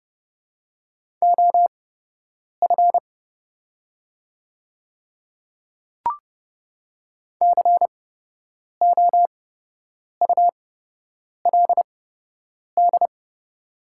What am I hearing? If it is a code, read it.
Morse code, heard as a keyed tone, about 30 wpm, spaced as for 8 wpm, 700 Hz. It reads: OF COULD
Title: 8x Character Spacing - Top 100 Words - 30wpm